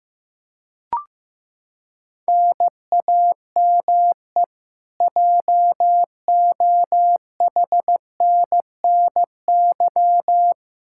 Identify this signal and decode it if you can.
Morse code, heard as a keyed tone, about 15 wpm, 700 Hz. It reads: NAME JOHNNY
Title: Practice QSO - Exchange 1 - Beginning - 15wpm